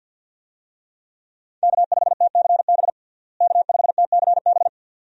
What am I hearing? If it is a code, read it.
Morse code, heard as a keyed tone, about 51 wpm, 700 Hz. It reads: K5TCB K5TCB
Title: USA Call Signs - 5 Characters - Repeated 2x - 50wpm